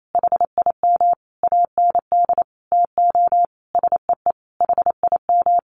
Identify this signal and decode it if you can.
Morse code, heard as a keyed tone, about 28 wpm, 700 Hz. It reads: HIM AND TO SEE HIM
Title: Sentences from Top 1400 Words 28wpm